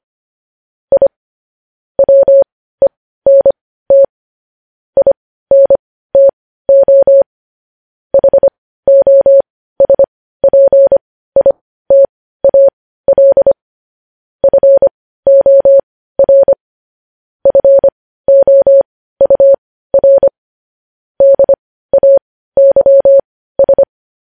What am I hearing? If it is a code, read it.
Morse code, heard as a keyed tone, about 25 wpm, spaced as for 15 wpm, 550 Hz. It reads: I WENT INTO HOSPITAL FOR FOUR DAYS
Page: news headlines in morse code at 15 wpm